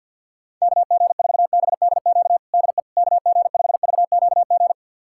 Morse code, heard as a keyed tone, about 51 wpm, 700 Hz. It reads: KG4BDQ DE KG5VQG
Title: USA Callsign DE Callsign 50wpm